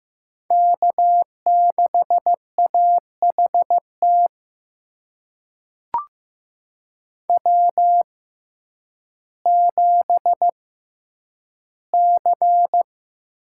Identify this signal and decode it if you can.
Morse code, heard as a keyed tone, about 15 wpm, spaced as for 9 wpm, 700 Hz. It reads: K6AHT W7C
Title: USA Call Signs - 5 Characters - 6x Character Spacing - 15wpm